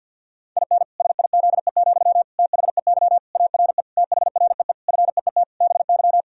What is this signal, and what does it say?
Morse code, heard as a keyed tone, about 52 wpm, 700 Hz. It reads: IN SIZE, THEY ARE THREE FEET BY
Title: Sentences from Top 1000 Words - Review Part 2 - 50wpm